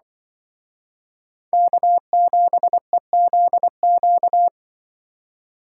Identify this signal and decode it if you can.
Morse code, heard as a keyed tone, about 23 wpm, 700 Hz. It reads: K7EZQ
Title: Speed Racing - USA 1x3 Call Signs 20wpm